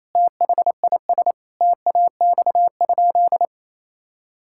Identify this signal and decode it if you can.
Morse code, heard as a keyed tone, about 28 wpm, 700 Hz. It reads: THIS TAX?
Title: Sentences from Top 1000 Words - Review Part 5 - 28wpm